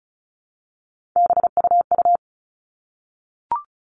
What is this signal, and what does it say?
Morse code, heard as a keyed tone, about 34 wpm, 700 Hz. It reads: BUU
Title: All Cut Numbers - 3-Digit 35wpm